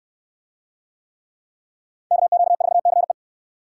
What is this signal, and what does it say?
Morse code, heard as a keyed tone, about 62 wpm, 690 Hz. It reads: KQ4CE